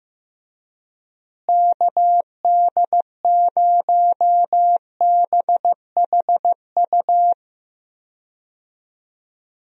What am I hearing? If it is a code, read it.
Morse code, heard as a keyed tone, about 15 wpm, 700 Hz. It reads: KD0BHU
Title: USA Call Signs - 6 Characters - Repeated 2x - 15wpm